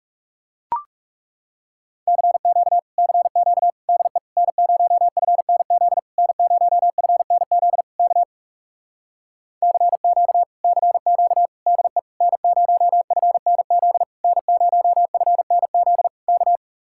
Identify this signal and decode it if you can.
Morse code, heard as a keyed tone, about 40 wpm, 700 Hz. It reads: CQ CQ DE N0FNZ N0FNZ K CQ CQ DE N0FNZ N0FNZ K